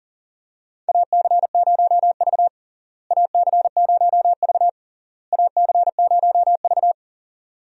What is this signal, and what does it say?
Morse code, heard as a keyed tone, about 40 wpm, 700 Hz. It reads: AC0V AC0V AC0V
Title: USA Call Signs - 4 Characters - Repeated 3x - 40wpm